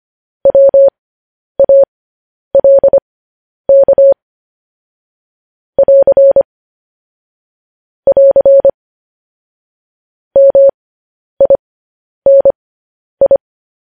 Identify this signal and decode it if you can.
Morse code, heard as a keyed tone, about 25 wpm, spaced as for 10 wpm, 550 Hz. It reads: WALK <AR> <AR> MINI
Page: news headlines in morse code at 10 wpm